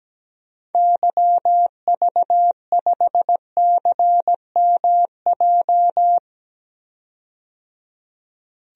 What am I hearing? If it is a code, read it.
Morse code, heard as a keyed tone, about 17 wpm, 700 Hz. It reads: YV5CMJ
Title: DXCC-100 Call Signs - 6 Characters - 17wpm